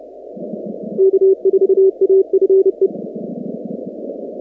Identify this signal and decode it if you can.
Morse code, heard as a keyed tone, about 29 wpm, 380 Hz. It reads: K4AFE
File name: contests / 2019-cw-wpx-kt5j-k5pi- -2330 0-46